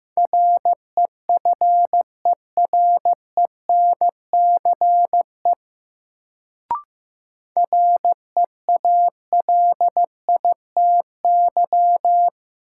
Morse code, heard as a keyed tone, about 15 wpm, 700 Hz. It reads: REFERENCE REALITY
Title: Sentences from Top 1200 Words - New Word Drill Alt - 15wpm